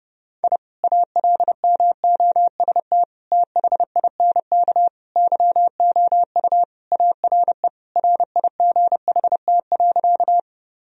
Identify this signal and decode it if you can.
Morse code, heard as a keyed tone, about 30 wpm, 700 Hz. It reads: I ALMOST THINK YOU ARE RIGHT.
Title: Sentences from Top 1000 Words - Review Part 4 - 30wpm